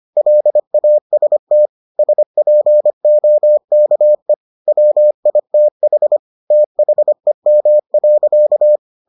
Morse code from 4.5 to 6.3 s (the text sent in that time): WITH